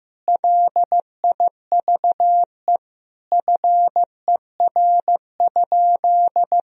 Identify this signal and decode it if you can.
Morse code, heard as a keyed tone, about 15 wpm, 700 Hz. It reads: LIVE FER?
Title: Sentences from Top 200 Words and Abbreviations 15wpm